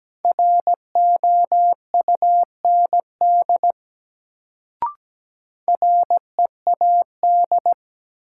Morse code, heard as a keyed tone, about 17 wpm, 700 Hz. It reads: ROUND READ
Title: Sentences from Top 200 Words - New Word Drill - 17wpm